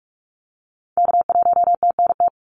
Morse code, heard as a keyed tone, about 46 wpm, 700 Hz. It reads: K1TNT
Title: NA SOTA Activator Callsigns 45wpm